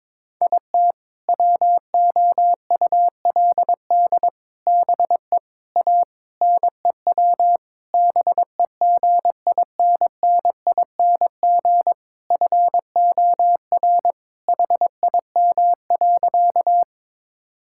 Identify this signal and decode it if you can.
Morse code, heard as a keyed tone, about 22 wpm, 700 Hz. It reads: IT WOULD BE A NEW BEGINNING FOR HIM.
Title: Sentences from Top 1000 Words - Review Part 5 - 22wpm